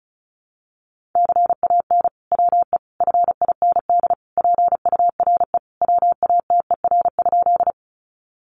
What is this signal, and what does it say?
Morse code, heard as a keyed tone, about 34 wpm, 700 Hz. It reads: CAN WE FIND PURE WATER?
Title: Sentences from Top 1500 Words 35wpm